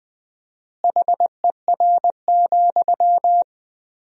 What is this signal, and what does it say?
Morse code, heard as a keyed tone, about 21 wpm, 700 Hz. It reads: HER,
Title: 4x Word-Spacing - Sentences from Top 200 Words - 20wpm